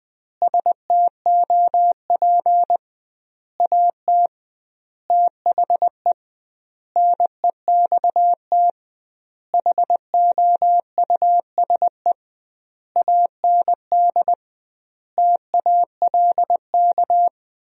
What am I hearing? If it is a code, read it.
Morse code, heard as a keyed tone, about 20 wpm, 700 Hz. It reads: STOP AT THE NEXT HOUSE AND TALK
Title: 2x Word-Spacing - Sentences from Top 400 Words - 20wpm